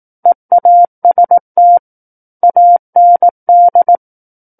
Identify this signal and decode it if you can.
Morse code, heard as a keyed tone, about 18 wpm, 700 Hz. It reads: EAST AND
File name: Book One / 09-Chapter 9 - 18 WPM